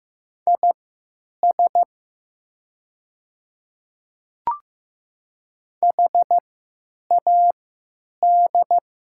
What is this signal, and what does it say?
Morse code, heard as a keyed tone, about 15 wpm, spaced as for 9 wpm, 700 Hz. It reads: IS HAD
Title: 3x Character Spacing - Top 200 Words - 15wpm